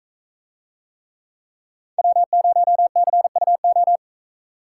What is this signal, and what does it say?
Morse code, heard as a keyed tone, about 41 wpm, 700 Hz. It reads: W0CUO